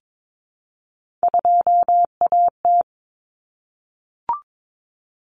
Morse code, heard as a keyed tone, about 22 wpm, 700 Hz. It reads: 2AT
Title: Common Cut Numbers - 3-Digit 22wpm